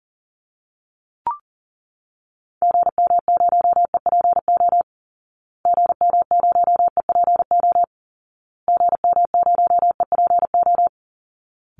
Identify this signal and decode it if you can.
Morse code, heard as a keyed tone, about 40 wpm, 700 Hz. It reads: GM0EPO GM0EPO GM0EPO
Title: Call Signs Repeated 2x - 40wpm